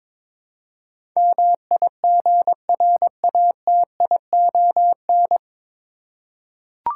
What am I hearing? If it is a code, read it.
Morse code, heard as a keyed tone, about 22 wpm, 700 Hz. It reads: MIGRATION
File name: 9 Letter Words - First 3 Letters Given - 22wpm